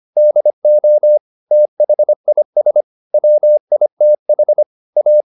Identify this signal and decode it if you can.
Morse code, heard as a keyed tone, about 25 wpm, 600 Hz. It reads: DO THIS WITH A
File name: the old man and the sea part 1 25wpm